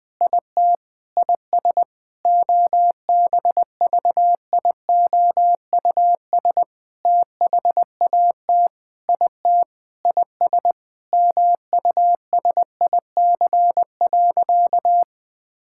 Morse code, heard as a keyed tone, about 20 wpm, 700 Hz. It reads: IT IS OBVIOUS THAT IT IS MUSIC.
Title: Sentences from Top 1600 Words 20wpm